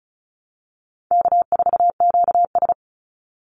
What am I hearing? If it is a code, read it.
Morse code, heard as a keyed tone, about 34 wpm, 700 Hz. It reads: K4QS
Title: NA SOTA Activator Callsigns 35wpm